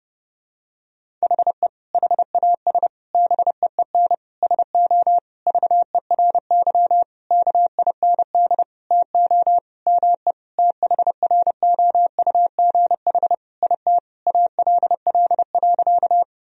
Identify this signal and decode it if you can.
Morse code, heard as a keyed tone, about 30 wpm, 700 Hz. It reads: HE HAS BEEN SO VERY KIND TO ME THROUGH IT ALL.
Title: Sentences from Top 1000 Words - Review Part 1 - 30wpm